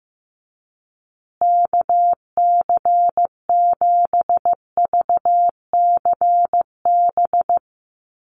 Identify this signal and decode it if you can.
Morse code, heard as a keyed tone, about 15 wpm, 700 Hz. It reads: KC7VCB
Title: USA Call Signs Repeated 2x 15wpm